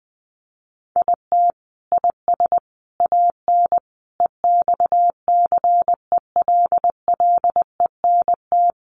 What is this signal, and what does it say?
Morse code, heard as a keyed tone, about 20 wpm, 700 Hz. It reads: IT IS AN EXCELLENT
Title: Sentences from Top 1000 Words 20wpm